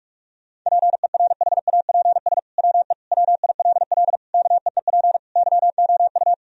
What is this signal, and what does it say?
Morse code, heard as a keyed tone, about 41 wpm, 700 Hz. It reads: PERHAPS WE WILL KEEP YOU
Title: Sentences from Top 1000 Words - Review Part 3 - 45wpm